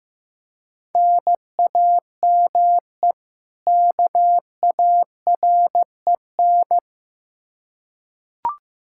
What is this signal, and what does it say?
Morse code, heard as a keyed tone, about 15 wpm, 700 Hz. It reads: NAME KAREN
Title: Practice QSO - Exchange 1 - Beginning - 15wpm